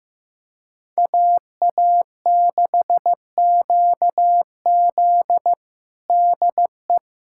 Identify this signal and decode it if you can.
Morse code, heard as a keyed tone, about 16 wpm, 700 Hz. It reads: AA6QZ DE
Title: USA Callsign DE Callsign - 5 Characters - 15wpm